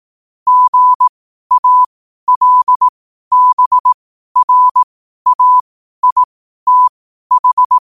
Morse code, heard as a keyed tone, about 18 wpm, spaced as for 12 wpm, 1000 Hz. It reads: GALBRAITH
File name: qotdfiles / 12wpm